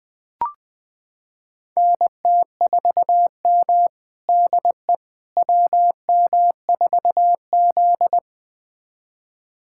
Morse code, heard as a keyed tone, about 20 wpm, 700 Hz. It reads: NT4M DE WM4Z
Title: USA Callsign DE Callsign - 4 Characters - 20wpm